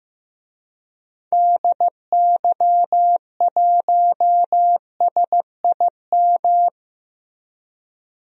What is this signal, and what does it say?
Morse code, heard as a keyed tone, about 15 wpm, 700 Hz. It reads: DY1SIM
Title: DXCC-100 Call Signs - 6 Characters - Repeated 3x - 15wpm